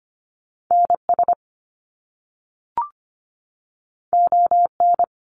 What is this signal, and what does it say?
Morse code, heard as a keyed tone, about 25 wpm, 700 Hz. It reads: NS ON
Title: Canadian Provinces and Territories 25wpm